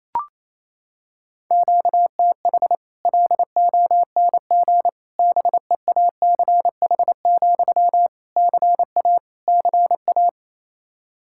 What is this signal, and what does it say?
Morse code, heard as a keyed tone, about 28 wpm, 700 Hz. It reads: QTH LONG BEACH, CA CA
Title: Practice QSO - Exchange 1 - Beginning - 28wpm